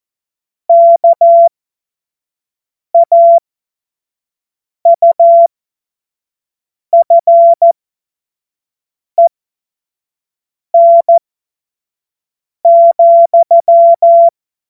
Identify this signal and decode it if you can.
Morse code, heard as a keyed tone, about 14 wpm, spaced as for 5 wpm, 680 Hz. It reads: KAUFEN,
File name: Witz-15-Juli-2025-25-F70